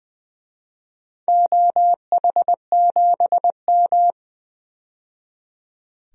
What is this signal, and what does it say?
Morse code, heard as a keyed tone, about 21 wpm, 700 Hz. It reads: OH7M
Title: DXCC-100 Call Signs - 4 Characters - 20wpm